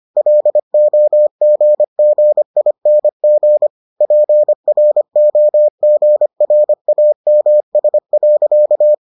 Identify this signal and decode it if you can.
Morse code, heard as a keyed tone, about 25 wpm, 600 Hz. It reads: LOGGING PROGRAMS.